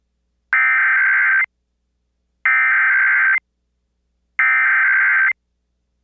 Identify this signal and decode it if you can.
Emergency Alert System (SAME) tones (header bursts), heard as harsh buzzing bursts. It is heard from 0.5 to 5.3 s.